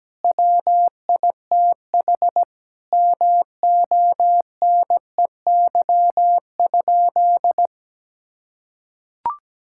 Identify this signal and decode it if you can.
Morse code, heard as a keyed tone, about 17 wpm, 700 Hz. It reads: WITH MONEY?
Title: Sentences from Top 1600 Words 17wpm